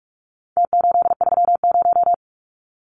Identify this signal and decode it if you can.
Morse code, heard as a keyed tone, about 44 wpm, 700 Hz. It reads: T830
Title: Numbers - Contest Style - 4-Digit 45wpm